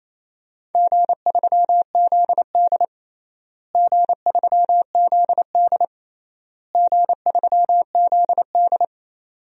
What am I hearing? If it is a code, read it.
Morse code, heard as a keyed tone, about 28 wpm, 700 Hz. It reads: G3ZD G3ZD G3ZD
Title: DXCC-100 Call Signs - 4 Characters - Repeated 3x - 28wpm